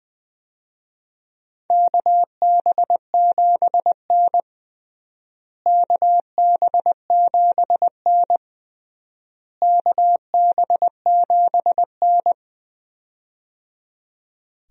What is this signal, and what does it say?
Morse code, heard as a keyed tone, about 20 wpm, 700 Hz. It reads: KB7N KB7N KB7N
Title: USA Call Signs - 4 Characters - Repeated 3x - 20wpm